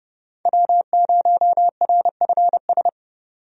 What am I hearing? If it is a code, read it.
Morse code, heard as a keyed tone, about 32 wpm, 700 Hz. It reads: W0RFS
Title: USA Call Signs - 5 Characters - 30wpm